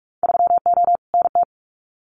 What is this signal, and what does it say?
Morse code, heard as a keyed tone, about 46 wpm, 700 Hz. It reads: 3O NT
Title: Winter Field Day Exchanges Encore 45wpm